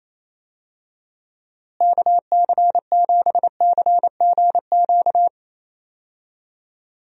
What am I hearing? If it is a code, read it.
Morse code, heard as a keyed tone, about 28 wpm, 700 Hz. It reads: KC7CGQ
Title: USA Call Signs - 6 Characters - Repeated 2x - 28wpm